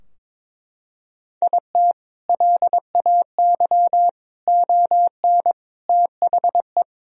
Morse code, heard as a keyed tone, about 22 wpm, 700 Hz. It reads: IT LAY ON THE
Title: Sentences from Top 1000 Words - Review Part 2 - 22wpm